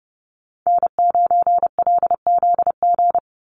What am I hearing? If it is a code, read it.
Morse code, heard as a keyed tone, about 30 wpm, 700 Hz. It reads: N9LZG
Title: USA Call Signs 30wpm